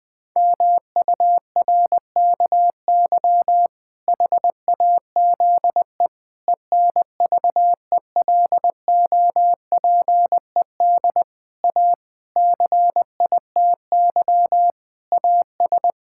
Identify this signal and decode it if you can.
Morse code, heard as a keyed tone, about 19 wpm, 700 Hz. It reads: MURKY HAZE ENVELOPED A CITY AS
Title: Pangram Challenge v2 - 20wpm